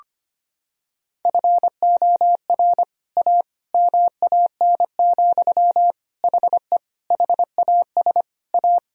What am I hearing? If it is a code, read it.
Morse code, heard as a keyed tone, about 25 wpm, 700 Hz. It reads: FOR A MAN, HE HAS A
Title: Sentences from Top 1000 Words - Review Part 2 - 25wpm